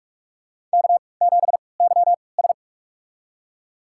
Morse code, heard as a keyed tone, about 46 wpm, spaced as for 25 wpm, 700 Hz. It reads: K7YS
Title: USA Call Signs - 4 Characters - 3x Character Spacing - 45wpm